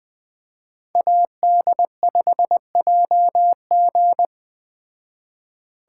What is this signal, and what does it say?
Morse code, heard as a keyed tone, about 20 wpm, 700 Hz. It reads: AD5JG
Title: Speed Racing - USA 2x2 Call Signs 17wpm